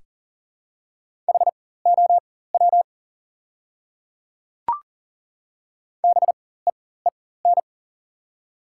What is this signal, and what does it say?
Morse code, heard as a keyed tone, about 40 wpm, spaced as for 19 wpm, 690 Hz. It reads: HOW BEEN
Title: 4x Character Spacing - Top 100 Words - 40wpm